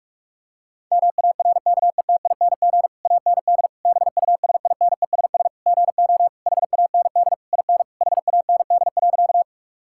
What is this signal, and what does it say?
Morse code, heard as a keyed tone, about 41 wpm, 700 Hz. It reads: MARKETING AND BUSINESS GO HAND IN HAND.
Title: Sentences from Top 1000 Words - Review Part 5 - 45wpm